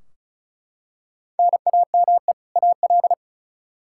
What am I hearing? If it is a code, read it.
Morse code, heard as a keyed tone, about 36 wpm, 700 Hz. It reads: NAME AL
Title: Practice QSO - Exchange 1 - Beginning - 35wpm